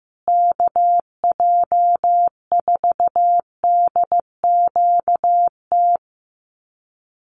Morse code, heard as a keyed tone, about 15 wpm, 700 Hz. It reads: KJ4DQT
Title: USA Call Signs Repeated 3x 15wpm